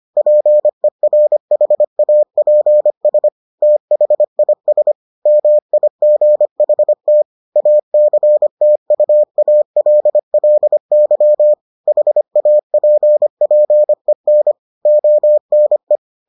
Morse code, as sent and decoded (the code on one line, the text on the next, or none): .--. .-. .... .- .--. ... / - .... .. ... / -- .. --. .... - / .- -.-. - ..- .- .-.. .-.. -.-- / .... .- .--. .--. -. / --- -. .
PERHAPS THIS MIGHT ACTUALLY HAPPEN ONE